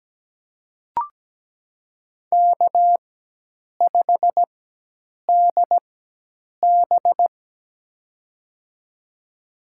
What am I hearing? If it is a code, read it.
Morse code, heard as a keyed tone, about 17 wpm, spaced as for 8 wpm, 700 Hz. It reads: K5DB
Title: USA Call Signs - 4 Characters - 4x Character Spacing - 17wpm